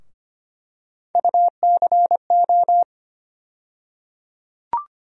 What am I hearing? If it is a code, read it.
Morse code, heard as a keyed tone, about 25 wpm, 700 Hz. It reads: UCO
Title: USA Call Sign Postfixes - 3 Letters 25wpm